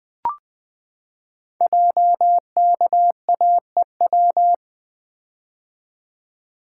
Morse code, heard as a keyed tone, about 20 wpm, 700 Hz. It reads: JKAEW
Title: Code Groups - 5 Letters - 20wpm